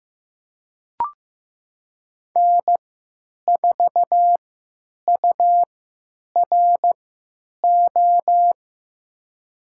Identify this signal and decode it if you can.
Morse code, heard as a keyed tone, about 15 wpm, spaced as for 9 wpm, 700 Hz. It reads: N4URO
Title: USA Call Signs - 5 Characters - 3x Character Spacing - 15wpm